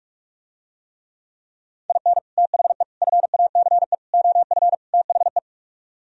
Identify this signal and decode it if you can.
Morse code, heard as a keyed tone, about 47 wpm, 700 Hz. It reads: IN THE FACE OF THE